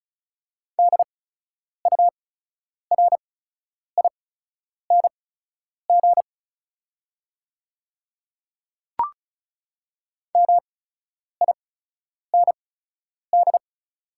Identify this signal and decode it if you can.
Morse code, heard as a keyed tone, about 36 wpm, spaced as for 10 wpm, 690 Hz. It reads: DURING MIND
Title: 8x Character Spacing - Top 500 Words - 35wpm